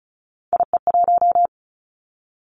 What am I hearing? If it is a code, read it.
Morse code, heard as a keyed tone, about 36 wpm, 700 Hz. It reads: IE1
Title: Three Challenging Letter-Number Combinations 35wpm